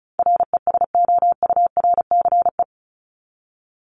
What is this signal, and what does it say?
Morse code, heard as a keyed tone, about 34 wpm, 700 Hz. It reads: RESOURCE
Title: Top 1001-1250 Words 35wpm